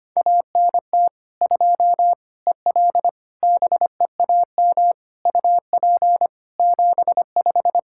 Morse code, heard as a keyed tone, about 25 wpm, 700 Hz. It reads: ANT 2 EL BEAM UP 75
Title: K1USN SST QSO Practice 25wpm